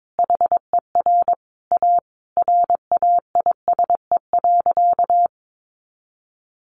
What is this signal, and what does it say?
Morse code, heard as a keyed tone, about 22 wpm, 700 Hz. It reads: HER A RAISE.
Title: Easy Phrases 22wpm